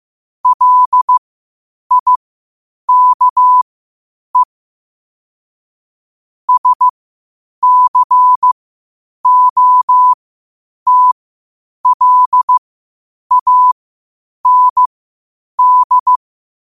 Morse code, heard as a keyed tone, about 15 wpm, spaced as for 8 wpm, 1000 Hz. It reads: LIKE SCOTLAND
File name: qotdfiles / 8wpm